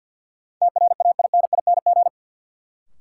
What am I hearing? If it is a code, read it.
Morse code, heard as a keyed tone, about 51 wpm, 700 Hz. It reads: TRAINING
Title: Sentences from Top 1000 Words - Review Part 4 - New Word Drill - 50wpm